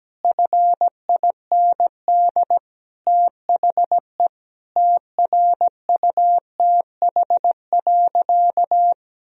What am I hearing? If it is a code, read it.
Morse code, heard as a keyed tone, about 17 wpm, 700 Hz. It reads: FIND THE TRUTH.
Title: Sentences from Top 1000 Words - Review Part 3 - 17wpm